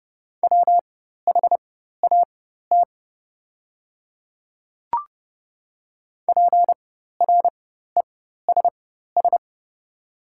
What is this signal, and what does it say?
Morse code, heard as a keyed tone, about 30 wpm, spaced as for 14 wpm, 700 Hz. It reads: WHAT PRESS
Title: 4x Character Spacing - Top 300 Words - 30wpm